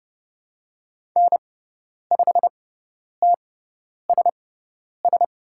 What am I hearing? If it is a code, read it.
Morse code, heard as a keyed tone, about 30 wpm, spaced as for 10 wpm, 700 Hz. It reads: N5TSS
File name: Lesson 16g - Letter c - Call Signs 30wpm